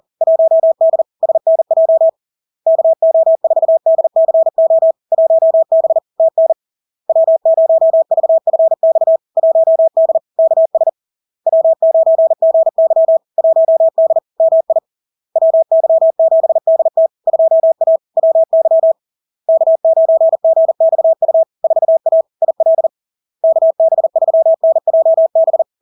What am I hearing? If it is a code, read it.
Morse code, heard as a keyed tone, about 40 wpm, 650 Hz. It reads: 1D SNJ KO4DCO 1B TN W0VFX 1D KS W9GY 1D MI WY7DT 2A WY K9GXU 4A IL KB3NJB
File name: LARK-FD-2022-Log-in-CW-40wpm-650-Hz